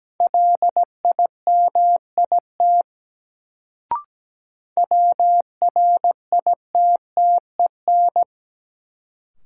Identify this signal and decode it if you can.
Morse code, heard as a keyed tone, about 17 wpm, 700 Hz. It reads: LIMIT WRITTEN
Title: Sentences from Top 1500 Words - New Word Drill Alt - 17wpm